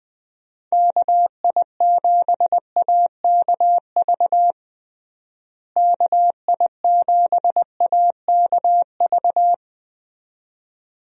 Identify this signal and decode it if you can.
Morse code, heard as a keyed tone, about 20 wpm, 700 Hz. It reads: KI7AKV KI7AKV
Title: USA Call Signs - 6 Characters - Repeated 2x - 20wpm